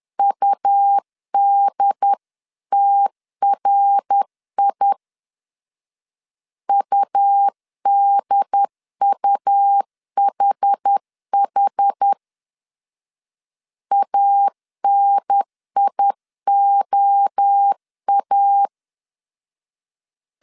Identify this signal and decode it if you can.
Morse code, heard as a keyed tone, about 10 wpm, 800 Hz. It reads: UDTRI UDUHH ANIOA
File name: 10wpm